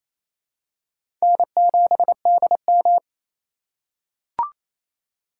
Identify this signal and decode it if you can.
Morse code, heard as a keyed tone, about 27 wpm, 700 Hz. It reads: N7DM
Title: USA Call Signs - 4 Characters - 4x Character Spacing - 28wpm